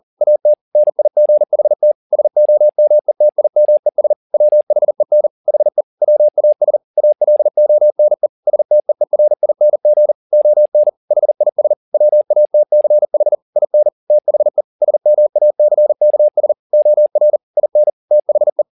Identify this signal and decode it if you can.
Morse code, heard as a keyed tone, about 40 wpm, 600 Hz. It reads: AT NIGHT SOMETIMES WHEN HE WAS ALONE STEERING ON HIS WATCH IN THE SMACKS OR IN THE